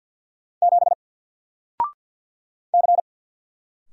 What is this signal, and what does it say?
Morse code, heard as a keyed tone, about 48 wpm, 700 Hz. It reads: 7 C
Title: Single Character (Letter:Number:Punctuation:Prosign) 50wpm